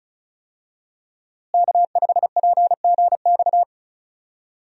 Morse code, heard as a keyed tone, about 34 wpm, 700 Hz. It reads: K5PGX
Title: USA Call Signs - 5 Characters - Repeated 2x - 35wpm